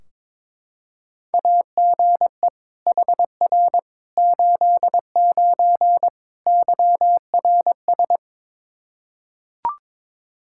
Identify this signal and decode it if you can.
Morse code, heard as a keyed tone, about 21 wpm, 700 Hz. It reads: AGE HR 89 YRS
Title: Practice QSO - Exchange 3 - More Info - 22wpm